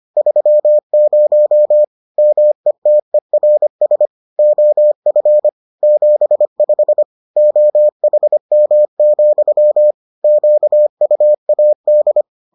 Morse code, heard as a keyed tone, about 25 wpm, 600 Hz. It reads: 30 METERS OF 75 OHM, QUAD